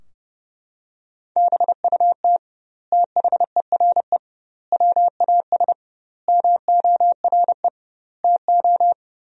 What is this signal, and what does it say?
Morse code, heard as a keyed tone, about 30 wpm, 700 Hz. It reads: BUT THERE WAS MORE TO
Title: 2x Word-Spacing - Sentences from Top 100 Words - 30wpm